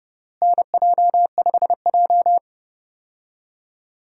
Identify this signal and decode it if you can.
Morse code, heard as a keyed tone, about 32 wpm, 700 Hz. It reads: NJ5J
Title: USA Call Signs - 4 Characters - Repeated 3x - 30wpm